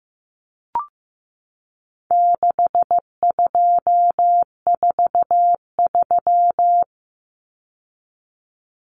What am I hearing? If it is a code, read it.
Morse code, heard as a keyed tone, about 15 wpm, 700 Hz. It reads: 6243
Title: Numbers - 4-Digit 15wpm